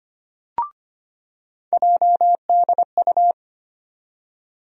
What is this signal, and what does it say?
Morse code, heard as a keyed tone, about 25 wpm, 700 Hz. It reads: JDU
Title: USA Call Sign Postfixes - 3 Letters 25wpm